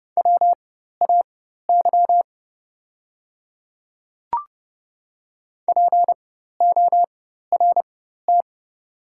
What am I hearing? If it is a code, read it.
Morse code, heard as a keyed tone, about 30 wpm, spaced as for 14 wpm, 690 Hz. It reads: WAY PORT
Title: 4x Character Spacing - Top 300 Words - 30wpm